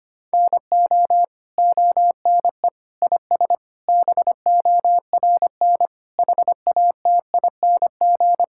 Morse code, heard as a keyed tone, about 25 wpm, 700 Hz. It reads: NO ONE IS BORN HATING
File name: practice / 25wpm / Sentences with 9-Word Phrases - 25wpm